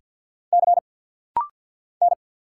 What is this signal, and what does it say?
Morse code, heard as a keyed tone, about 50 wpm, 700 Hz. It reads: C N
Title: Single Letter-Number - Rapid-Fire 50wpm